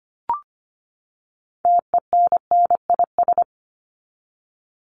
Morse code, heard as a keyed tone, about 25 wpm, 700 Hz. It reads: TENNIS
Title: Top 1501-1750 Words 25wpm